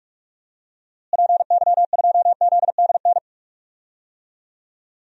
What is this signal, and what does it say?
Morse code, heard as a keyed tone, about 46 wpm, 700 Hz. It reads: PY2ZDN
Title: DXCC-100 Call Signs - 6 Characters - Repeated 3x - 45wpm